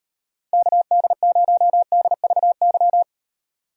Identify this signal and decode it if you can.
Morse code, heard as a keyed tone, about 38 wpm, 700 Hz. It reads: KD0DVY